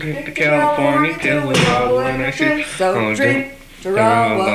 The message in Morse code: none